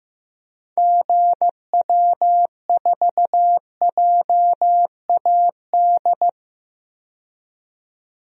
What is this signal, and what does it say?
Morse code, heard as a keyed tone, about 16 wpm, 700 Hz. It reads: GW4JAD
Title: DXCC-100 Call Signs - 6 Characters - 15wpm